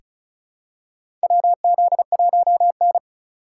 Morse code, heard as a keyed tone, about 34 wpm, 700 Hz. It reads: WZ1N
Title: USA Call Signs - 3x Character Spacing - 35wpm